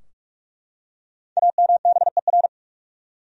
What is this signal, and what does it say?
Morse code, heard as a keyed tone, about 44 wpm, 700 Hz. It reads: AMBER
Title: USA Common Names - Max 5 Letters - 45wpm